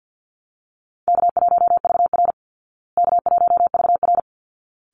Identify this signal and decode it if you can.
Morse code, heard as a keyed tone, about 48 wpm, 700 Hz. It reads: K1VR K1VR
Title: Easy Call Signs Repeated 2x - 50wpm